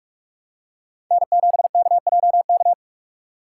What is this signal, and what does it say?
Morse code, heard as a keyed tone, about 46 wpm, 700 Hz. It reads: N7KJK
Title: USA Call Signs - 5 Characters - 2x Character Spacing - 45wpm